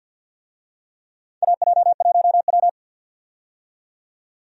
Morse code, heard as a keyed tone, about 51 wpm, 700 Hz. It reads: AJ1W
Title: USA Call Signs - 4 Characters - 50wpm